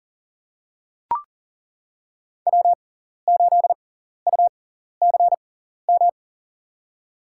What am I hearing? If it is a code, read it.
Morse code, heard as a keyed tone, about 40 wpm, spaced as for 14 wpm, 700 Hz. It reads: W8UCM
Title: USA Call Signs - 6x Character Spacing - 40wpm